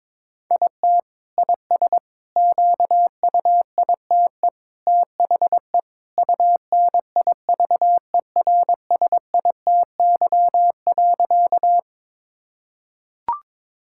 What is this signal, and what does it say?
Morse code, heard as a keyed tone, about 22 wpm, 700 Hz. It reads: IT IS QUITE THE UNIVERSITY.
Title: Sentences from Top 1000 Words - Review Part 5 - 22wpm